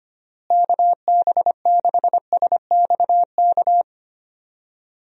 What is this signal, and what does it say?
Morse code, heard as a keyed tone, about 25 wpm, 700 Hz. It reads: KB6SXK
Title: USA Call Signs - 6 Characters - Repeated 2x - 25wpm